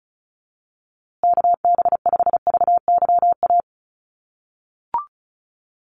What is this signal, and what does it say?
Morse code, heard as a keyed tone, about 34 wpm, 700 Hz. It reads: KB5VYA
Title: USA Call Signs Repeated 2x 35wpm